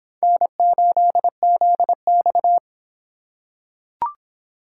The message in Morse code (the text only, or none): N8ZX